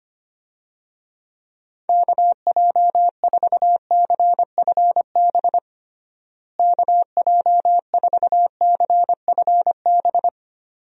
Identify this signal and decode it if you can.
Morse code, heard as a keyed tone, about 25 wpm, 700 Hz. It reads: KJ4CFB KJ4CFB
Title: USA Call Signs - 6 Characters - Repeated 2x - 25wpm